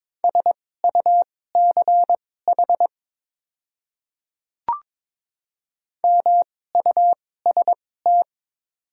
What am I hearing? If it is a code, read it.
Morse code, heard as a keyed tone, about 22 wpm, spaced as for 16 wpm, 700 Hz. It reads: SUCH MUST
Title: 2x Character Spacing - Top 500 Words - 22wpm